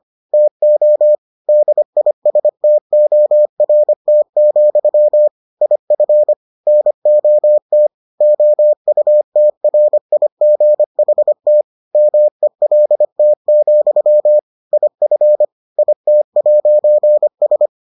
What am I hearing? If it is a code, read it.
Morse code, heard as a keyed tone, about 25 wpm, 600 Hz. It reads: TO DISTORT, IF NOT OUTRIGHT MELT, IF IT'S